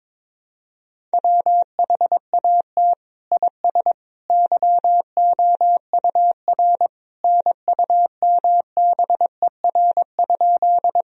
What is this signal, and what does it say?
Morse code, heard as a keyed tone, about 21 wpm, 700 Hz. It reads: WHAT IS YOUR NUMBER?
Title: Sentences from Top 1000 Words - Review Part 1 - 22wpm